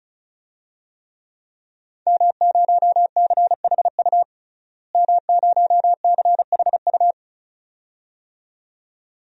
Morse code, heard as a keyed tone, about 34 wpm, 700 Hz. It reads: M0CHU M0CHU
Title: DXCC-100 Call Signs - 5 Characters - Repeated 2x - 35wpm